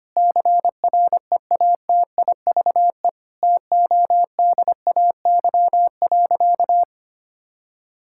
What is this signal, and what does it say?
Morse code, heard as a keyed tone, about 25 wpm, 700 Hz. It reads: CREATIVE TODAY.
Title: Sentences from Top 1000 Words - Review Part 5 - 25wpm